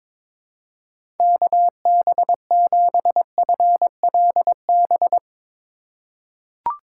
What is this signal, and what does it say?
Morse code, heard as a keyed tone, about 22 wpm, 700 Hz. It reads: KB7FLB
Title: USA Call Signs - 6 Characters - 3x Character Spacing - 22wpm